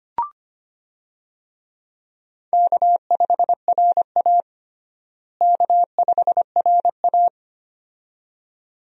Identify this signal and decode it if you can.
Morse code, heard as a keyed tone, about 25 wpm, 700 Hz. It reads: K5RA K5RA
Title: USA Call Signs - 4 Characters - Repeated 2x - 25wpm